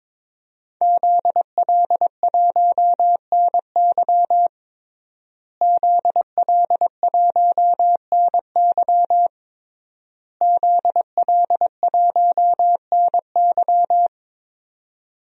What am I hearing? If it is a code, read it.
Morse code, heard as a keyed tone, about 22 wpm, 700 Hz. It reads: ZL1NY ZL1NY ZL1NY
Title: DXCC-100 Call Signs - 5 Characters - Repeated 3x - 22wpm